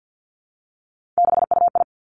Morse code, heard as a keyed tone, about 50 wpm, 700 Hz. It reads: 6UI
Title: Three Challenging Letter-Number Combinations 50wpm